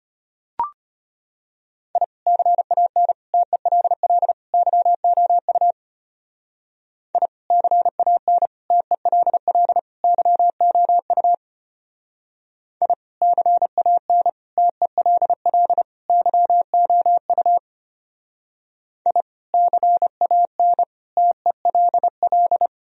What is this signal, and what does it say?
Morse code, heard as a keyed tone, about 32 wpm, 700 Hz. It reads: I CAN TELL YOU I CAN TELL YOU I CAN TELL YOU I CAN TELL